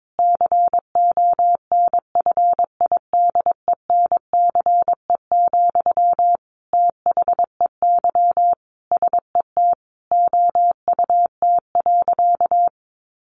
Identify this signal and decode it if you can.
Morse code, heard as a keyed tone, about 22 wpm, 700 Hz. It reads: CONFIDENCE, THEY SET OUT.
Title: Sentences from Top 1500 Words 22wpm